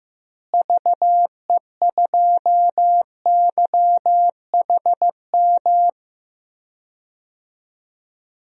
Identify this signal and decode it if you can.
Morse code, heard as a keyed tone, about 16 wpm, 700 Hz. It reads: VE2YHM
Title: DXCC-100 Call Signs - 6 Characters - Repeated 2x - 15wpm